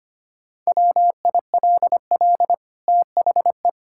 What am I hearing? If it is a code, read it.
Morse code, heard as a keyed tone, about 25 wpm, 700 Hz. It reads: WILL THE
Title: Sentences from Top 1000 Words - Review Part 5 - 25wpm